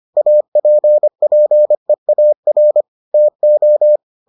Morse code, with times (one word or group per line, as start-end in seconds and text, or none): 0.2-2.8 APPEAR
3.1-4.0 TO